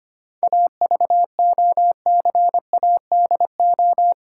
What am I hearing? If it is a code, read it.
Morse code, heard as a keyed tone, about 25 wpm, 700 Hz. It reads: AVOCADO
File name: practice / 25wpm / Difficult Words with Hints - 25wpm